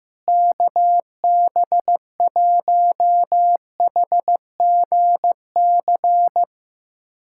Morse code, heard as a keyed tone, about 15 wpm, 700 Hz. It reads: KB1HGC
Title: USA Callsign DE Callsign 15wpm